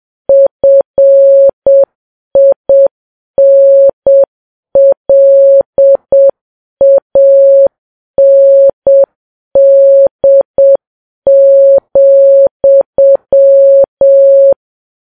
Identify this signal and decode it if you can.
Morse code, heard as a keyed tone, about 7 wpm, 550 Hz. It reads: FINLAND,
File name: Files Web tls / Morse on9fv / morse7kps